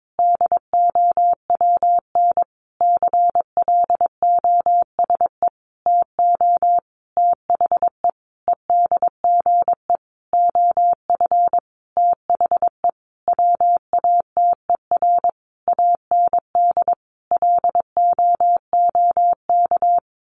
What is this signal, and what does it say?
Morse code, heard as a keyed tone, about 22 wpm, 700 Hz. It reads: DOWN CLOSE TO THE EDGE OF THE WATER AND LOOK
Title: Sentences from Top 1300 Words 22wpm